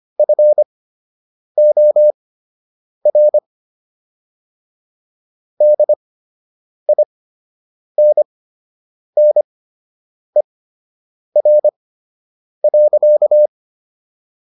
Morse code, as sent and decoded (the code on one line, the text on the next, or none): ..-. --- .-. / -.. .. -. -. .-. .-.-.-
FOR DINNER.